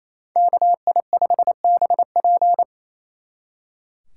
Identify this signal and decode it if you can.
Morse code, heard as a keyed tone, about 34 wpm, 700 Hz. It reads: KI5BP
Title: Speed Racing - USA 2x2 Call Signs 28wpm